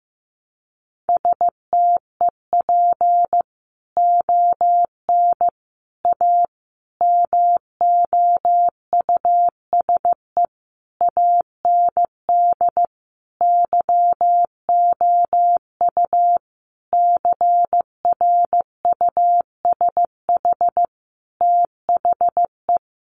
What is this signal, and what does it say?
Morse code, heard as a keyed tone, about 15 wpm, 700 Hz. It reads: STEP ON A MOUSE AND YOU CRUSH THE
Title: A Sound Of Thunder 15wpm